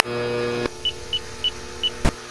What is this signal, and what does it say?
There is no code.